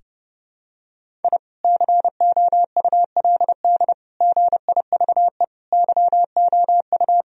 Morse code, heard as a keyed tone, about 30 wpm, 700 Hz. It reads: I COULD GIVE YOU
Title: Sentences from Top 1000 Words - Review Part 1 - 30wpm